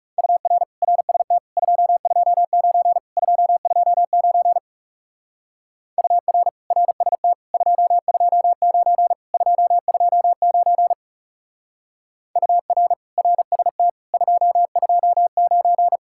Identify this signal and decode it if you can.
Morse code, heard as a keyed tone, about 37 wpm, 700 Hz. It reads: UR RST 229 229 UR RST 229 229 UR RST 229
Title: Speed-Racing Common QSO Phrases - 30wpm